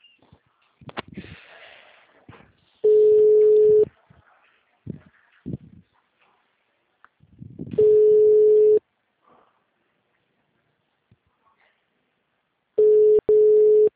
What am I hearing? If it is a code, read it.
Ringback tone (Europe, 425 Hz).